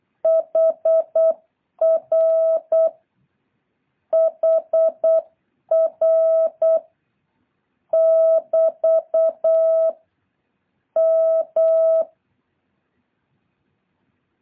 Morse code, heard as a keyed tone, about 8 wpm, 650 Hz. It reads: HR HR <BT> M
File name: jh8cbh / 2023cw